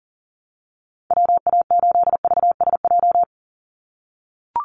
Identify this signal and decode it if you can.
Morse code, heard as a keyed tone, about 40 wpm, 700 Hz. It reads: WA8VSJ
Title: Call Signs - 40wpm